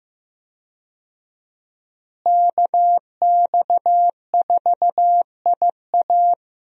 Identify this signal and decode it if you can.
Morse code, heard as a keyed tone, about 15 wpm, 700 Hz. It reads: KX4IA
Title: USA Call Signs - 5 Characters - Repeated 3x - 15wpm